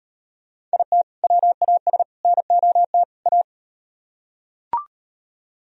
Morse code, heard as a keyed tone, about 38 wpm, 700 Hz. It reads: IT WAS NOT A